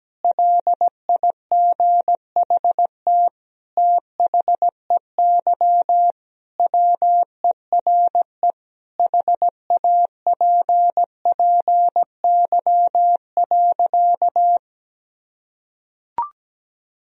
Morse code, heard as a keyed tone, about 17 wpm, 700 Hz. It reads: LIGHT THEY WERE HAPPY.
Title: Sentences from Top 1000 Words - Review Part 1 - 17wpm